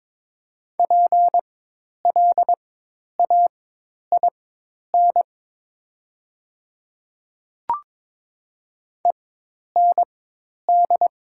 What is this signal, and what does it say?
Morse code, heard as a keyed tone, about 21 wpm, spaced as for 10 wpm, 700 Hz. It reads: PLAIN END
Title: 4x Character Spacing - Top 500 Words - 22wpm